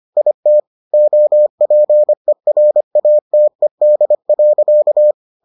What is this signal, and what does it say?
Morse code, heard as a keyed tone, about 25 wpm, 600 Hz. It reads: IT OPERATED.